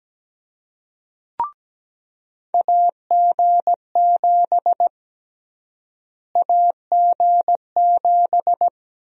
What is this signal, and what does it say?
Morse code, heard as a keyed tone, about 17 wpm, 700 Hz. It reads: AG7 AG7
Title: USA Call Sign Prefixes - Repeated 2x - 17wpm